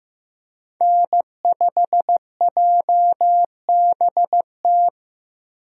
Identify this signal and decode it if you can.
Morse code, heard as a keyed tone, about 16 wpm, 700 Hz. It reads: N5JBT
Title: USA Call Signs - 5 Characters - Repeated 3x - 15wpm